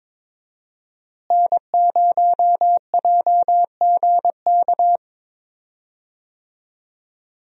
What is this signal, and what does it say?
Morse code, heard as a keyed tone, about 21 wpm, 700 Hz. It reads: N0JGK
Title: USA Call Signs - 5 Characters - Repeated 2x - 22wpm